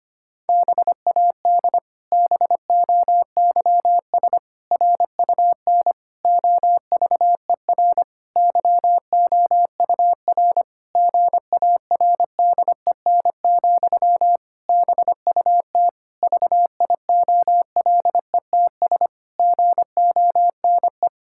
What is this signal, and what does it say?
Morse code, heard as a keyed tone, about 25 wpm, 700 Hz. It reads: BAD BOYS RUN OVER YOUR GARDEN, BUT VIOLETS GONE
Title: Pangram Challenge v2 - 25wpm